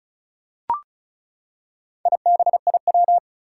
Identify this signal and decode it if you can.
Morse code, heard as a keyed tone, about 36 wpm, 690 Hz. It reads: IBIW
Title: Four Challenging Letter-Number Combinations 35wpm